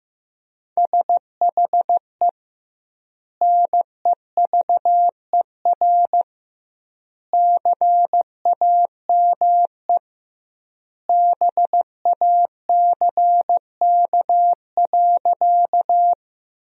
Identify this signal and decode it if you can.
Morse code, heard as a keyed tone, about 15 wpm, 700 Hz. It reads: SHE NEVER CAME BACK.
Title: 2x Word-Spacing - Sentences from Top 300 Words - 15wpm